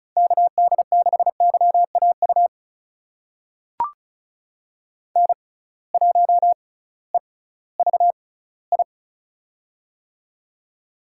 Morse code, heard as a keyed tone, about 36 wpm, spaced as for 19 wpm, 700 Hz. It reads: KD6YAU N1EVI
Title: USA Call Signs - 6x Character Spacing - 35wpm